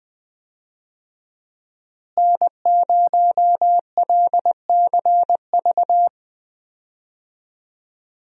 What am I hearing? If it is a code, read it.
Morse code, heard as a keyed tone, about 20 wpm, 700 Hz. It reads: N0LCV
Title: USA Call Signs - 5 Characters - Repeated 3x - 20wpm